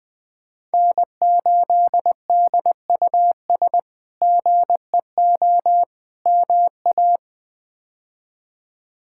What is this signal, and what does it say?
Morse code, heard as a keyed tone, about 20 wpm, 700 Hz. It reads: N8DUS GEO MA
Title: K1USN SST QSO Practice - 20wpm